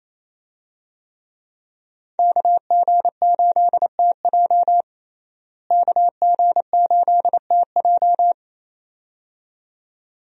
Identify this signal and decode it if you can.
Morse code, heard as a keyed tone, about 28 wpm, 700 Hz. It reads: KG8TJ KG8TJ
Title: USA Call Signs - 5 Characters - Repeated 2x - 28wpm